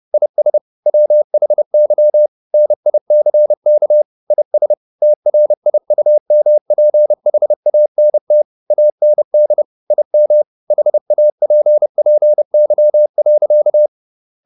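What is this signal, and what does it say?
Morse code, heard as a keyed tone, about 30 wpm, 600 Hz. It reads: IS WHY NICK IS TRIUMPHANT AND IM HAPPY.
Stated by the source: Morse code at 30 wpm, the novel Never by Ken Follett in English, from CW med Gnister.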